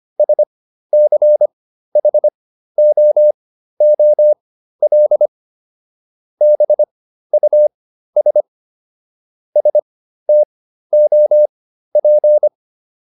Morse code, heard as a keyed tone, about 25 wpm, spaced as for 13 wpm, 600 Hz. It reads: SCHOOL BUS STOP